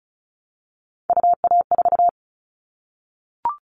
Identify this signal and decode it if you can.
Morse code, heard as a keyed tone, about 36 wpm, 700 Hz. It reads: UA4
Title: Callsign Prefixes for Top 25 DXCC - 35wpm